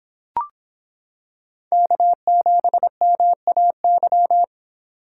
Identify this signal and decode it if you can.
Morse code, heard as a keyed tone, about 26 wpm, 700 Hz. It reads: K7MAY